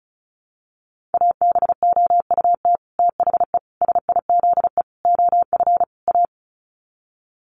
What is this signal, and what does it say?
Morse code, heard as a keyed tone, about 34 wpm, 700 Hz. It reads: ABOUT THE SIZE OF A
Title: Sets of 5 Words 35wpm